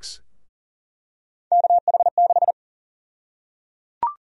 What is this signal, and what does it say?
Morse code, heard as a keyed tone, about 37 wpm, 700 Hz. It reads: KH6
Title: USA Call Sign Prefixes - 40wpm